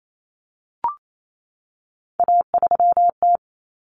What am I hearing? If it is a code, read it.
Morse code, heard as a keyed tone, about 28 wpm, 700 Hz. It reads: A3T
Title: Common Cut Numbers - 3-Digit 28wpm